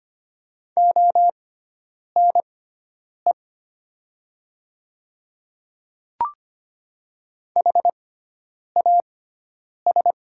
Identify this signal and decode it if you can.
Morse code, heard as a keyed tone, about 25 wpm, spaced as for 9 wpm, 700 Hz. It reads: ONE HAS
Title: 6x Character Spacing - Top 100 Words - 25wpm